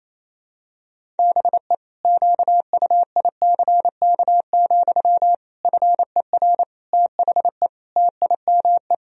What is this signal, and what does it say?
Morse code, heard as a keyed tone, about 28 wpm, 700 Hz. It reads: BE QUICK, FER THE TIME
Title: Sentences from Top 500 Words and Abbreviations 28wpm